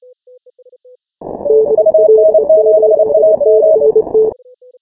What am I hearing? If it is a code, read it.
Morse code, heard as a keyed tone, about 29 wpm, 630 Hz. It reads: UN7AM
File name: final / W1UJ-168-140-MorseRunner